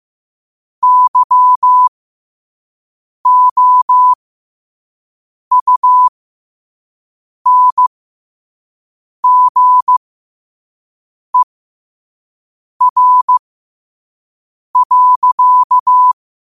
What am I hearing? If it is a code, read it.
Morse code, heard as a keyed tone, about 15 wpm, spaced as for 5 wpm, 1000 Hz. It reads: YOUNGER.
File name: qotdfiles / 5wpm